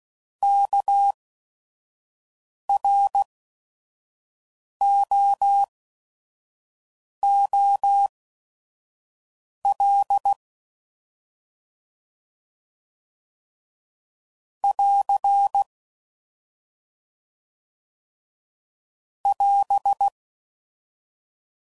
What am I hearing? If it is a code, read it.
Morse code, heard as a keyed tone, about 16 wpm, spaced as for 5 wpm, 800 Hz. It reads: KROOL <AR> <AS>